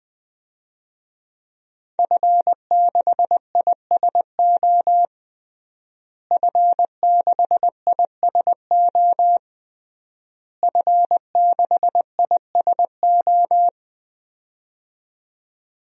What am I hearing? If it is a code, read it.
Morse code, heard as a keyed tone, about 20 wpm, 700 Hz. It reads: F6ISO F6ISO F6ISO
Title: DXCC-100 Call Signs - 5 Characters - Repeated 3x - 20wpm